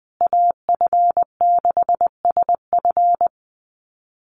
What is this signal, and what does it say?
Morse code, heard as a keyed tone, about 20 wpm, 700 Hz. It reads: AF6SF
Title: USA Call Signs Repeated 3x 20wpm